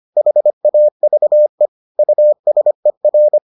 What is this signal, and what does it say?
Morse code, heard as a keyed tone, about 25 wpm, 600 Hz. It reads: HAVE USER